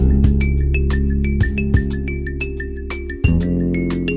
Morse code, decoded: none